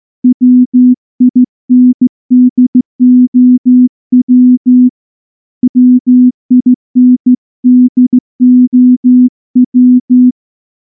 Morse code, heard as a keyed tone, about 13 wpm, 250 Hz. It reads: WINDOW WINDOW
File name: morse-training-15WPM